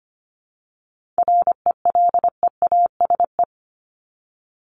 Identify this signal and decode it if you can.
Morse code, heard as a keyed tone, about 25 wpm, 700 Hz. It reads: RELEASE
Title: Top 1001-1250 Words 25wpm